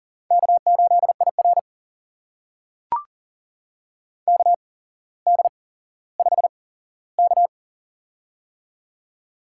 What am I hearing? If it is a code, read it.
Morse code, heard as a keyed tone, about 39 wpm, spaced as for 17 wpm, 700 Hz. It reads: K8IR KD5K
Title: USA Call Signs - 4 Characters - 8x Character Spacing - 40wpm